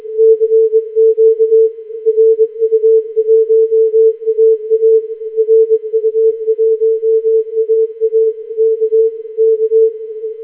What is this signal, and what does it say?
Morse code, heard as a keyed tone, about 21 wpm, 440 Hz. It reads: CQ RU1AA RU1AA K K
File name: audiofilter / eme example